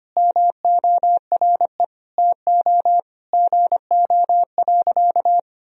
Morse code, heard as a keyed tone, about 25 wpm, 700 Hz. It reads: MORE TO GO.
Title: Sentences from Top 1000 Words - Review Part 2 - 25wpm